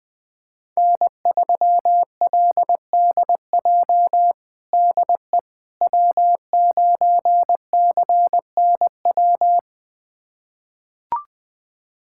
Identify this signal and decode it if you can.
Morse code, heard as a keyed tone, about 20 wpm, 700 Hz. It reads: N3LDJ DE W9CNW
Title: USA Callsign DE Callsign - 5 Characters - 20wpm